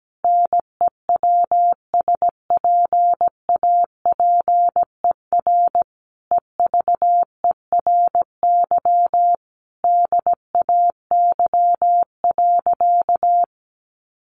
Morse code, heard as a keyed tone, about 17 wpm, 700 Hz. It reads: NEWSPAPER EVERY DAY.
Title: Sentences from Top 1100 Words 17wpm